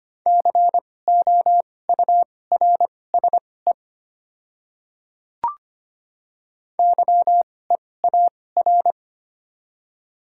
Morse code, heard as a keyed tone, about 25 wpm, spaced as for 18 wpm, 700 Hz. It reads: COURSE YEAR
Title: 2x Character Spacing - Top 500 Words - 25wpm